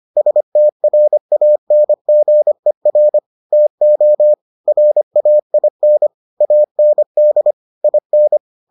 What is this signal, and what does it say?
Morse code, heard as a keyed tone, about 25 wpm, 600 Hz. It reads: STRANGER TO RAIN AND IN